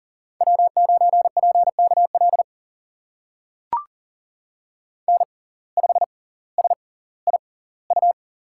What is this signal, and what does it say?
Morse code, heard as a keyed tone, about 40 wpm, spaced as for 20 wpm, 700 Hz. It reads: W9PKL N5SIU
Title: USA Call Signs - 5 Characters - 6x Character Spacing - 40wpm